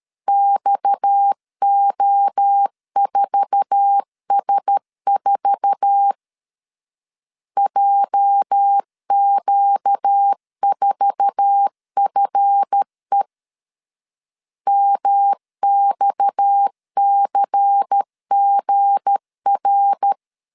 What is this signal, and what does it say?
Morse code, heard as a keyed tone, about 13 wpm, 800 Hz. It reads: XO4S4 JQ4FE MXCGR
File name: LEZIONI / 8 YJ45 / 8 YJ4 12wpm 2151